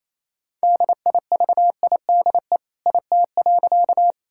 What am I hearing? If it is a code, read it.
Morse code, heard as a keyed tone, about 28 wpm, 700 Hz. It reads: DIVIDE IT.
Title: Sentences from Top 1600 Words 28wpm